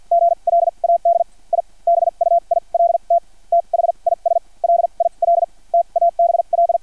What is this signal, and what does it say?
Morse code, heard as a keyed tone, about 50 wpm, spaced as for 38 wpm, 680 Hz. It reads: GLAD I BUILT THIS LIL TABL
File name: archived / CRSnet